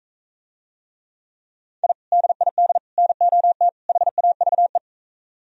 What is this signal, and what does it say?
Morse code, heard as a keyed tone, about 44 wpm, 700 Hz. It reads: I DID NOT HAVE